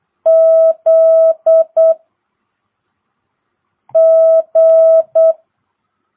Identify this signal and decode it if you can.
Morse code, heard as a keyed tone, about 8 wpm, 650 Hz. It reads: Z G
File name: jh8cbh / 2022cw